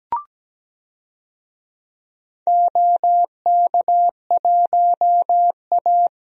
Morse code, heard as a keyed tone, about 17 wpm, 700 Hz. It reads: OK1A
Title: DXCC-100 Call Signs - 4 Characters - 17wpm